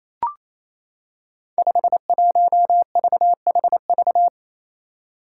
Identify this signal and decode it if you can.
Morse code, heard as a keyed tone, about 28 wpm, 700 Hz. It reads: 51VHV
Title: Five Challenging Letter-Number Combinations 28wpm